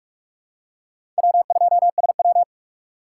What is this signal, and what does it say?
Morse code, heard as a keyed tone, about 44 wpm, 700 Hz. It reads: W2SW
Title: USA Call Signs - 4 Characters - 6x Character Spacing - 45wpm